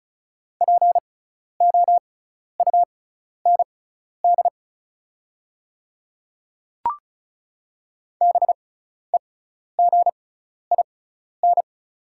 Morse code, heard as a keyed tone, about 34 wpm, spaced as for 12 wpm, 690 Hz. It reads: POUND BEGIN
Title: 6x Character Spacing - Top 500 Words - 35wpm